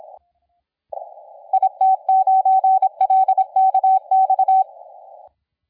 Morse code, heard as a keyed tone, about 26 wpm, 760 Hz. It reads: IT9LKX
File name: la4c / sac-cw-2017 / CONTEST AUDIO 1100Z